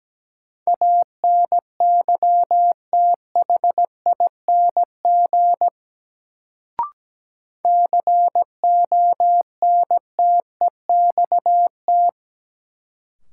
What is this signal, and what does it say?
Morse code, heard as a keyed tone, about 17 wpm, 700 Hz. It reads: ANYTHING CONTEXT
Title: Sets of 4 Words - New Word Drill - 17wpm